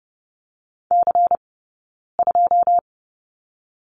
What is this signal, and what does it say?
Morse code, heard as a keyed tone, about 30 wpm, 700 Hz. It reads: C 2
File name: Lesson 16d - Letter c - Two Characters 30wpm